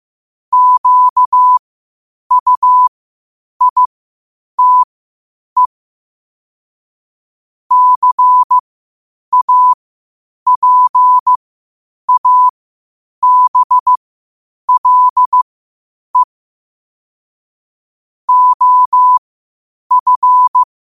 Morse code, heard as a keyed tone, about 15 wpm, spaced as for 8 wpm, 1000 Hz. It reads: QUITE CAPABLE OF
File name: qotdfiles / 8wpm